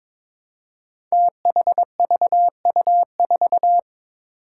Morse code, heard as a keyed tone, about 22 wpm, 700 Hz. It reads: THVU4
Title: Five Challenging Letter-Number Combinations 22wpm